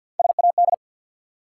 Morse code, heard as a keyed tone, about 50 wpm, 700 Hz. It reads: SAD
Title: Three Letter Words - 50wpm